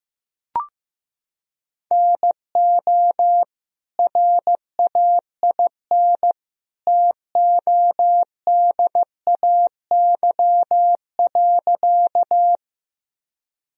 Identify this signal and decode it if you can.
Morse code, heard as a keyed tone, about 15 wpm, 700 Hz. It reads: NO RAIN TODAY.
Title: Sentences from Top 1000 Words - Review Part 1 - 15wpm